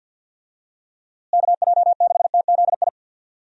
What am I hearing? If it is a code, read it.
Morse code, heard as a keyed tone, about 53 wpm, 700 Hz. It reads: KJ6TZI
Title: USA Call Signs - 6 Characters - Repeated 2x - 50wpm